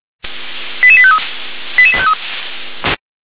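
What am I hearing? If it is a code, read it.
Five-tone selective call (ZVEI), heard as quick runs of five tones, tones of about 70 ms. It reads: 88853 88853
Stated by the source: ZVEI five-tone selective call received on an SDR.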